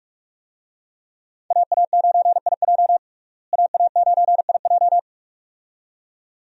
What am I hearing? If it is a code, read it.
Morse code, heard as a keyed tone, about 46 wpm, 700 Hz. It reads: AA9IJ AA9IJ
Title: USA Call Signs - 5 Characters - Repeated 2x - 45wpm